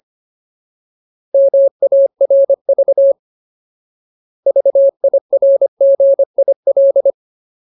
Morse code, heard as a keyed tone, about 25 wpm, 550 Hz. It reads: MARV VIRGIL
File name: Ultimate Word List / 25WPM Ultimate